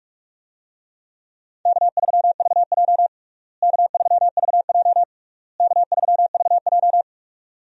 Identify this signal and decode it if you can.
Morse code, heard as a keyed tone, about 46 wpm, 700 Hz. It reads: K3VJ K3VJ K3VJ
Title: USA Call Signs - 4 Characters - Repeated 3x - 45wpm